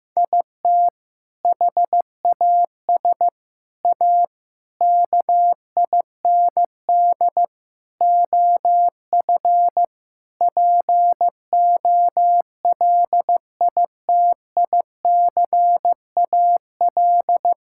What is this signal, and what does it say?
Morse code, heard as a keyed tone, about 15 wpm, 700 Hz. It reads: IT HAS A KIND OF POLITICAL
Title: Sentences from Top 1000 Words - Review Part 5 - 15wpm